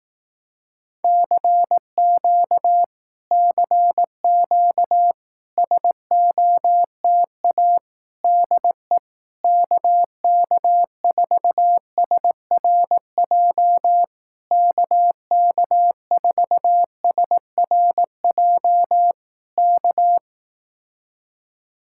Morse code, heard as a keyed tone, about 18 wpm, 700 Hz. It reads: CQ CQ SOTA DE KK4SRJ KK4SRJ K
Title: Speed-Racing Common QSO Phrases Encore - 15wpm